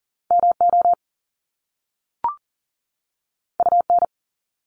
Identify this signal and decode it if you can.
Morse code, heard as a keyed tone, about 43 wpm, 700 Hz. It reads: MO UN
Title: Top Two Letter Combinations in Words 40wpm